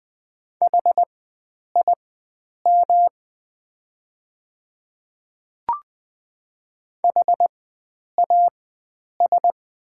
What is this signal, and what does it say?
Morse code, heard as a keyed tone, about 20 wpm, spaced as for 9 wpm, 700 Hz. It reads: HIM HAS
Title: 4x Character Spacing - Top 100 Words - 20wpm